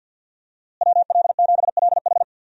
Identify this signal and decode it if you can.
Morse code, heard as a keyed tone, about 52 wpm, 700 Hz. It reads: WL7LH
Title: USA Call Signs - 5 Characters - Repeated 2x - 50wpm